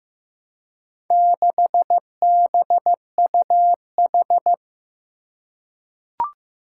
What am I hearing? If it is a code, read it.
Morse code, heard as a keyed tone, about 15 wpm, 700 Hz. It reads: 6BUH
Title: Four Challenging Letter-Number Combinations 15wpm